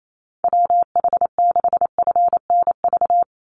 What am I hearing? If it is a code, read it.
Morse code, heard as a keyed tone, about 28 wpm, 700 Hz. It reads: WH6FNV
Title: USA Call Signs 28wpm